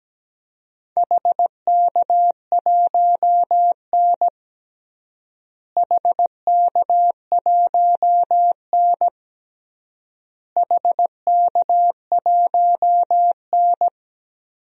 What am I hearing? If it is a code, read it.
Morse code, heard as a keyed tone, about 17 wpm, 700 Hz. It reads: HK1N HK1N HK1N
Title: DXCC-100 Call Signs - 4 Characters - Repeated 3x - 17wpm